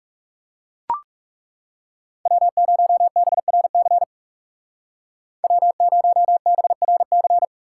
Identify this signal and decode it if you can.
Morse code, heard as a keyed tone, about 37 wpm, 700 Hz. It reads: W0BRC W0BRC
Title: Speed Racing - USA 1x3 Call Signs 30wpm